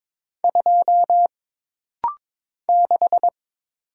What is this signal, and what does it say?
Morse code, heard as a keyed tone, about 22 wpm, 700 Hz. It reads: 2 6
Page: This rapid-fire timing is geared to push your ICR (Instant Character Recognition) skills.